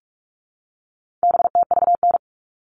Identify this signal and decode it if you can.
Morse code, heard as a keyed tone, about 44 wpm, 700 Hz. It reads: BTVN
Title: All Cut Numbers - 4-Digit 45wpm